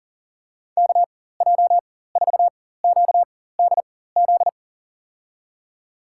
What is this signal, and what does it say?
Morse code, heard as a keyed tone, about 40 wpm, spaced as for 19 wpm, 700 Hz. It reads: KJ4QDZ
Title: USA Call Signs - 4x Character Spacing - 40wpm